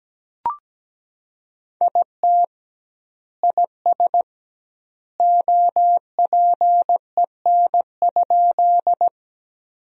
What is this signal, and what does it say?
Morse code, heard as a keyed tone, about 17 wpm, 700 Hz. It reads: IT IS OPEN?
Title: 2x Word-Spacing - Sentences from Top 300 Words - 17wpm